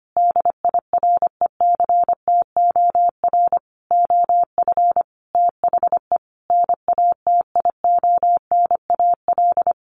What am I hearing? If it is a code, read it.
Morse code, heard as a keyed tone, about 25 wpm, 700 Hz. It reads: DIRECTOR OF THE NATIONAL
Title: Sets of 5 Words Encore2 25wpm